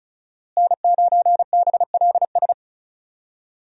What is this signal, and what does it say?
Morse code, heard as a keyed tone, about 36 wpm, 700 Hz. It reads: N9BLS
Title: USA Call Signs - 5 Characters - Repeated 2x - 35wpm